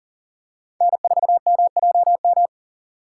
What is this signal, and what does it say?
Morse code, heard as a keyed tone, about 40 wpm, 700 Hz. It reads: N4MJM
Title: USA Call Signs - 2x Character Spacing - 40wpm